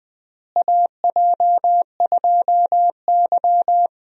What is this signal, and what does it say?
Morse code, heard as a keyed tone, about 20 wpm, 700 Hz. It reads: AJ2Y
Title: USA Call Signs - 4 Characters - Repeated 3x - 20wpm